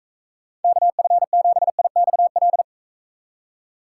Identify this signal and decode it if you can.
Morse code, heard as a keyed tone, about 39 wpm, 700 Hz. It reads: KF7IXL